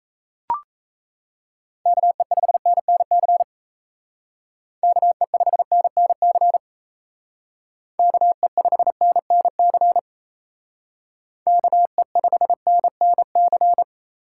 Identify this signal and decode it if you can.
Morse code, heard as a keyed tone, about 33 wpm, 700 Hz. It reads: KE5NNC KE5NNC KE5NNC KE5NNC
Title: Speed Racing - USA 2x3 Call Signs 28wpm